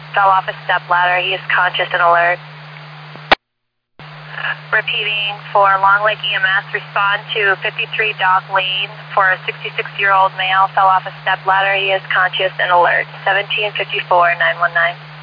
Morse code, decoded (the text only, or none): none